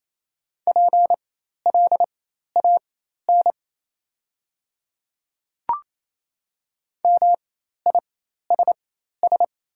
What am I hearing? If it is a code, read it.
Morse code, heard as a keyed tone, about 28 wpm, spaced as for 13 wpm, 700 Hz. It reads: PLAN MISS
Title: 4x Character Spacing - Top 500 Words - 28wpm